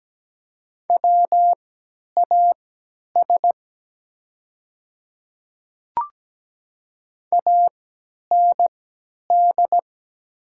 Morse code, heard as a keyed tone, about 17 wpm, spaced as for 10 wpm, 700 Hz. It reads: WAS AND
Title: 3x Character Spacing - Top 100 Words - 17wpm